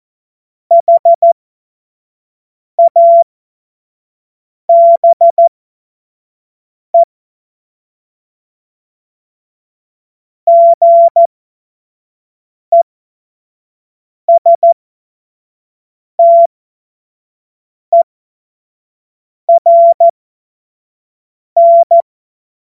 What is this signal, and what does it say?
Morse code, heard as a keyed tone, about 14 wpm, spaced as for 5 wpm, 680 Hz. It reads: HABE GESTERN
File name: Morsekurs / Witze / Witz 13. April 2021, 25 F70